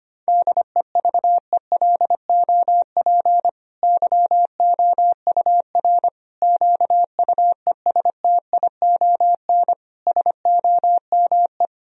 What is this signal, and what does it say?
Morse code, heard as a keyed tone, about 25 wpm, 700 Hz. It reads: DEVELOP YOUR QUESTION SOME
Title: Sentences from Top 1000 Words - Review Part 2 - 25wpm